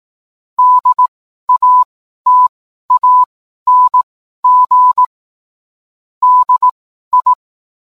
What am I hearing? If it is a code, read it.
Morse code, heard as a keyed tone, about 18 wpm, spaced as for 12 wpm, 1000 Hz. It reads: DATANG DI